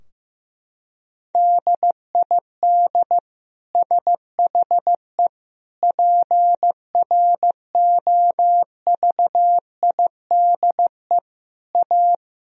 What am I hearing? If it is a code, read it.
Morse code, heard as a keyed tone, about 15 wpm, 700 Hz. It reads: DID SHE PROVIDE A
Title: Sentences from Top 1000 Words - Review Part 4 - 15wpm